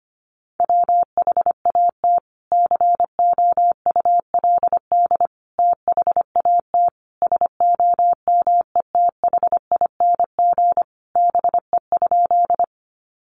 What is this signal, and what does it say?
Morse code, heard as a keyed tone, about 25 wpm, 700 Hz. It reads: WHAT COULD THAT SOMETHING BE?
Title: Sentences from Top 600 Words 25wpm